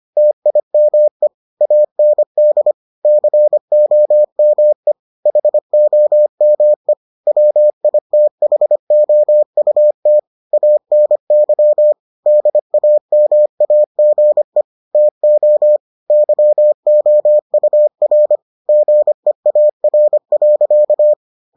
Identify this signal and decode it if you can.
Morse code, heard as a keyed tone, about 25 wpm, 600 Hz. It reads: TIME AND COME HOME WITHOUT ANY DAMAGE TO YOUR GEAR.